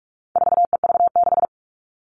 Morse code, heard as a keyed tone, about 44 wpm, 700 Hz. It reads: 4EV6
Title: All Cut Numbers - 4-Digit 45wpm